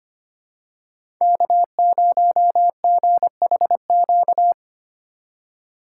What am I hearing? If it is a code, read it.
Morse code, heard as a keyed tone, about 25 wpm, 700 Hz. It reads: K0GHQ
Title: USA Call Signs - 5 Characters - 3x Character Spacing - 25wpm